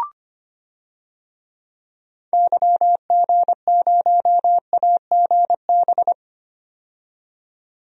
Morse code, heard as a keyed tone, about 25 wpm, 700 Hz. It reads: YG0AGB
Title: DXCC-100 Call Signs - 6 Characters - 25wpm